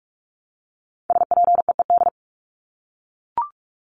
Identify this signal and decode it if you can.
Morse code, heard as a keyed tone, about 44 wpm, 700 Hz. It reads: SPEED
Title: Top 751-1000 Words 45wpm